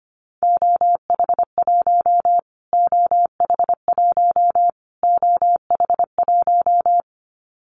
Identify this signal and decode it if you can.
Morse code, heard as a keyed tone, about 25 wpm, 700 Hz. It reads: OH1 OH1 OH1
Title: Callsign Prefixes for Top 25 DXCC - Repeated 3x - 25wpm